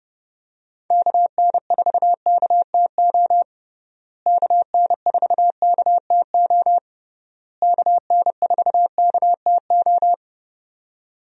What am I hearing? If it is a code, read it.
Morse code, heard as a keyed tone, about 30 wpm, 700 Hz. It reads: KN4KTO KN4KTO KN4KTO
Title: USA Call Signs - 6 Characters - Repeated 3x - 30wpm